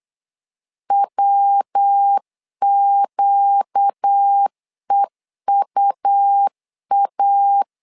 Morse code, heard as a keyed tone, about 8 wpm, 800 Hz. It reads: WQEUA
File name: 4 Q 8wpm